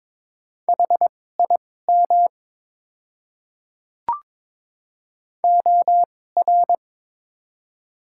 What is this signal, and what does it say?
Morse code, heard as a keyed tone, about 22 wpm, spaced as for 16 wpm, 700 Hz. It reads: HIM OR
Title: 2x Character Spacing - Top 100 Words - 22wpm